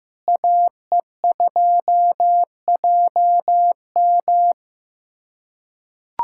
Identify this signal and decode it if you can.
Morse code, heard as a keyed tone, about 15 wpm, 700 Hz. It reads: AE2JM
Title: USA Callsign DE Callsign 15wpm